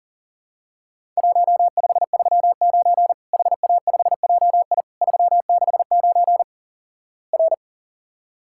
Morse code, heard as a keyed tone, about 44 wpm, 700 Hz. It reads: 1539 HA5JI 369 R
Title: Worked All Europe DX Contest - QTC Exchange - 40wpm